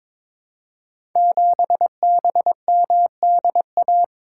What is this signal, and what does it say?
Morse code, heard as a keyed tone, about 22 wpm, 700 Hz. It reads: 7BMDA
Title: Code Groups - 5 Letter:Numbers - 22wpm